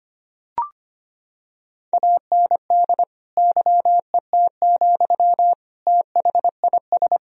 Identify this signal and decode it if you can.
Morse code, heard as a keyed tone, about 25 wpm, 700 Hz. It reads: AND YET, THIS
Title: Sentences from Top 1000 Words - Review Part 3 - 25wpm